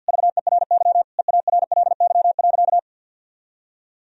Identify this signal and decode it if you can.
Morse code, heard as a keyed tone, about 52 wpm, 700 Hz. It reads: VERY EARLY.
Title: Sentences from Top 1000 Words - Review Part 2 - 50wpm